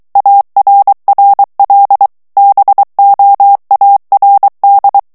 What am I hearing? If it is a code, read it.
Morse code, heard as a keyed tone, about 23 wpm, 800 Hz. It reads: ARRL BOARD
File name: ANGLAIS texte clair 3 vitesse 22 mots minute